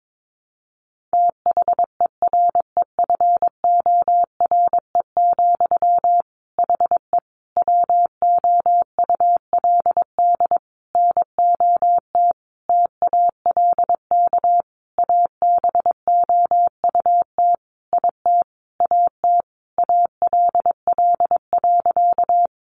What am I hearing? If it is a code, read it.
Morse code, heard as a keyed tone, about 22 wpm, 700 Hz. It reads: THEREFORE, HE WOULD NOT TALK ABOUT IT AT ALL.
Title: Sentences from Top 800 Words 22wpm